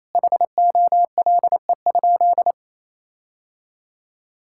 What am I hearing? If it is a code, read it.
Morse code, heard as a keyed tone, about 27 wpm, 700 Hz. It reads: HOLE?
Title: Sentences from Top 1600 Words 28wpm